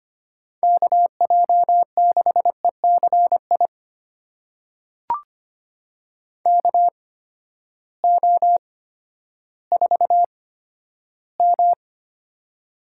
Morse code, heard as a keyed tone, about 25 wpm, spaced as for 12 wpm, 700 Hz. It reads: KJ6ECI KO4M
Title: USA Call Signs - 8x Character Spacing - 25wpm